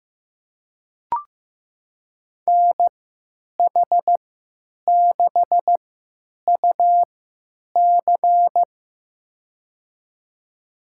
Morse code, heard as a keyed tone, about 15 wpm, spaced as for 9 wpm, 700 Hz. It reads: NH6UC
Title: USA Call Signs - 5 Characters - 3x Character Spacing - 15wpm